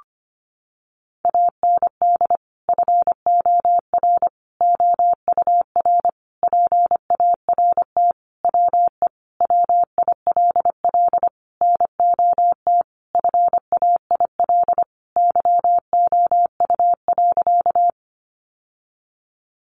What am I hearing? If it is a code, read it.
Morse code, heard as a keyed tone, about 25 wpm, 700 Hz. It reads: AND FOR OUR PART WE WILL NOT FAIL YOU.
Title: Sentences from Top 1100 Words 25wpm